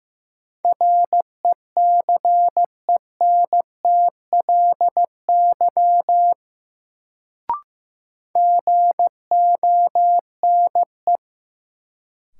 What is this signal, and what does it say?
Morse code, heard as a keyed tone, about 15 wpm, 700 Hz. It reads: RECENTLY GONE
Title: Sentences from Top 1200 Words - New Word Drill Alt - 15wpm